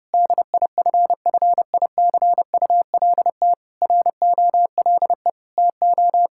Morse code, heard as a keyed tone, about 32 wpm, 700 Hz. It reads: DIFFICULT ROLE TO
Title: Sentences from Top 1000 Words - Review Part 6 - 30wpm